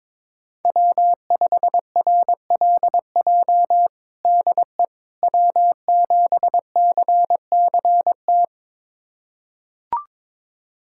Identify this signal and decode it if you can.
Morse code, heard as a keyed tone, about 22 wpm, 700 Hz. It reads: W5RLJ DE W7CCT
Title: USA Callsign DE Callsign - 5 Characters - 22wpm